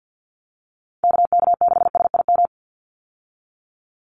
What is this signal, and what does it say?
Morse code, heard as a keyed tone, about 50 wpm, 700 Hz. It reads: KK6SIM
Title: USA Call Signs 50wpm